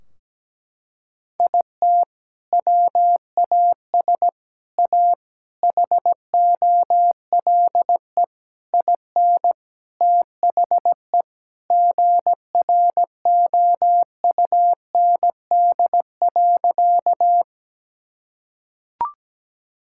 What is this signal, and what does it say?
Morse code, heard as a keyed tone, about 17 wpm, 700 Hz. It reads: IT WAS A HOLE IN THE GROUND.
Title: Sentences from Top 1600 Words 17wpm